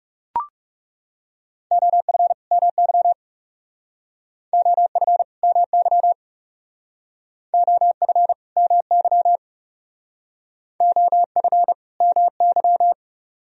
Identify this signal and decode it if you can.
Morse code, heard as a keyed tone, about 38 wpm, 700 Hz. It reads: OF MY OF MY OF MY OF MY